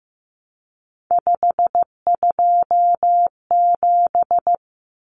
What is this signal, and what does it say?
Morse code, heard as a keyed tone, about 15 wpm, 700 Hz. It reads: 527
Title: Numbers - 3-Digit 15wpm